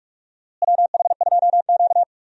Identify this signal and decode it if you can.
Morse code, heard as a keyed tone, about 46 wpm, 700 Hz. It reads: WH2Q
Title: USA Call Signs - 4 Characters - Repeated 3x - 45wpm